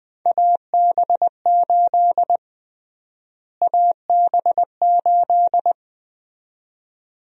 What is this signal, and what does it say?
Morse code, heard as a keyed tone, about 21 wpm, 700 Hz. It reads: AB8 AB8
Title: USA Call Sign Prefixes - Repeated 2x - 20wpm